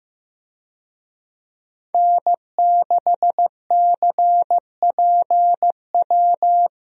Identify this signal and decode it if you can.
Morse code, heard as a keyed tone, about 16 wpm, 700 Hz. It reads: N6CPW
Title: USA Call Signs - 5 Characters - Repeated 2x - 15wpm